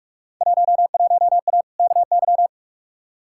Morse code, heard as a keyed tone, about 43 wpm, 700 Hz. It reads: 11A KY
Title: Field Day Exchanges Encore - 45wpm